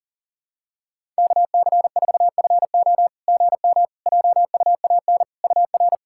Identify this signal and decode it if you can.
Morse code, heard as a keyed tone, about 39 wpm, 690 Hz. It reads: KC4FO GM JUAN UR
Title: SOTA QSO Practice 40wpm